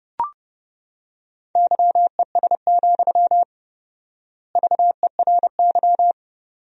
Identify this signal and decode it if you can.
Morse code, heard as a keyed tone, about 32 wpm, 700 Hz. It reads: YES, VERY
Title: 4x Word-Spacing - Sentences from Top 500 Words - 30wpm